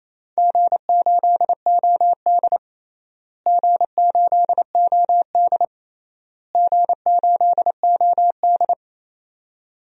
Morse code, heard as a keyed tone, about 27 wpm, 700 Hz. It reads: G8OD G8OD G8OD
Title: DXCC-100 Call Signs - 4 Characters - Repeated 3x - 28wpm